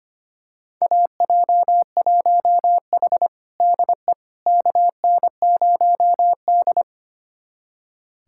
Morse code, heard as a keyed tone, about 25 wpm, 700 Hz. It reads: AJ1H DE KN0D
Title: USA Callsign DE Callsign - 4 Characters - 25wpm